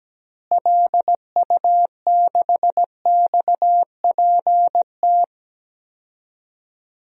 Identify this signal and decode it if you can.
Morse code, heard as a keyed tone, about 17 wpm, 700 Hz. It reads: LU6XPT
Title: DXCC-100 Call Signs - 6 Characters - Repeated 2x - 17wpm